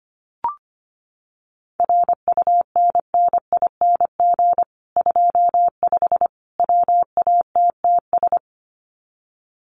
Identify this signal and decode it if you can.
Morse code, heard as a keyed tone, about 25 wpm, 700 Hz. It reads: RUNNING 25 WATTS
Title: Common QSO Phrases 25wpm